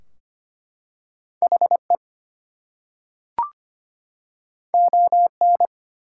Morse code, heard as a keyed tone, about 25 wpm, 700 Hz. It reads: HE ON
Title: Two Letter Words - 25wpm